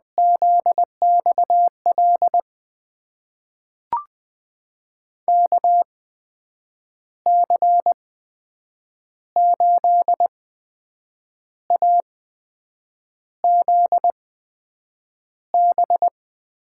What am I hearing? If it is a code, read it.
Morse code, heard as a keyed tone, about 20 wpm, spaced as for 7 wpm, 700 Hz. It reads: ZXL KC8AZB
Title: USA Call Signs - 6 Characters - 8x Character Spacing - 20wpm